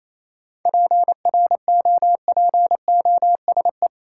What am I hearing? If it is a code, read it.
Morse code, heard as a keyed tone, about 28 wpm, 700 Hz. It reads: PROPOSE
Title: Seven Letter Words - 28wpm